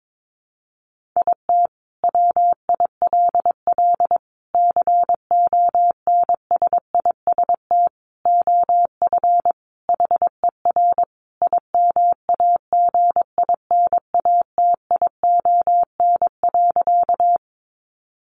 Morse code, heard as a keyed tone, about 22 wpm, 700 Hz. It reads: IT WILL CONSIST OF HER IMAGINATION.
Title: Sentences from Top 1400 Words 22wpm